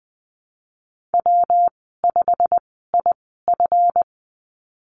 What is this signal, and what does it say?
Morse code, heard as a keyed tone, about 20 wpm, spaced as for 15 wpm, 700 Hz. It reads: W5IF
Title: 140 Lesson 23g - Letter f - Call Signs 20wpm